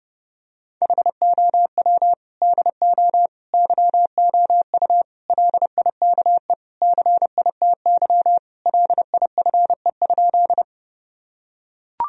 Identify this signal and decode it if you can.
Morse code, heard as a keyed tone, about 32 wpm, 700 Hz. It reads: HOW DO YOU LIKE CITY LIFE?
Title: Sentences from Top 1000 Words - Review Part 1 - 30wpm